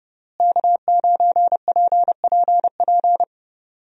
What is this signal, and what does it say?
Morse code, heard as a keyed tone, about 32 wpm, 700 Hz. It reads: K9PPP
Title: USA Call Signs - 5 Characters - Repeated 3x - 30wpm